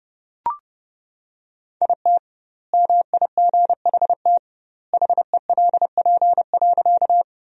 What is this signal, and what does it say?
Morse code, heard as a keyed tone, about 32 wpm, 700 Hz. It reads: IT MIGHT HELP.
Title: 2x Word-Spacing - Sentences from Top 300 Words - 30wpm